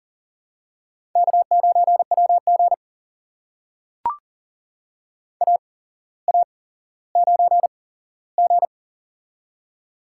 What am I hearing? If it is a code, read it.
Morse code, heard as a keyed tone, about 40 wpm, spaced as for 17 wpm, 700 Hz. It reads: K9WG AA9G
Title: USA Call Signs - 4 Characters - 8x Character Spacing - 40wpm